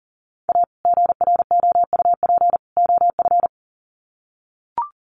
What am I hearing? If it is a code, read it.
Morse code, heard as a keyed tone, about 40 wpm, 700 Hz. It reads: A GROUP OF
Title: Sets of 3 Words 40wpm